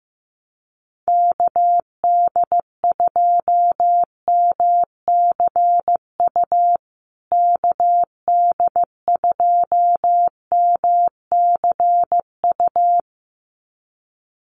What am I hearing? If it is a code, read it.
Morse code, heard as a keyed tone, about 15 wpm, 700 Hz. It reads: KD2MCU KD2MCU
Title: USA Call Signs Repeated 2x 15wpm